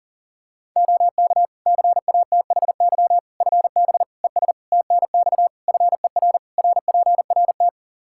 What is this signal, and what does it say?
Morse code, heard as a keyed tone, about 40 wpm, 700 Hz. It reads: OK CATHY FB ES TNX FER RPRT
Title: K1USN SST QSO Practice 40wpm